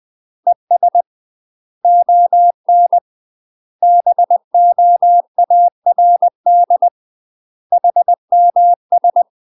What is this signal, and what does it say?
Morse code, heard as a keyed tone, about 20 wpm, 700 Hz. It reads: ES ON BOARD HMS
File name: UK MASTER QSOs TRAINING / 20WPM / UK MASTER QSOs 0030